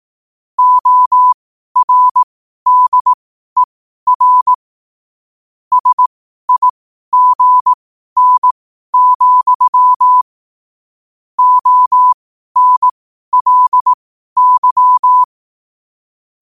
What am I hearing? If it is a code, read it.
Morse code, heard as a keyed tone, about 18 wpm, spaced as for 12 wpm, 1000 Hz. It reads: ORDER SIGN, ONLY
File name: qotdfiles / 12wpm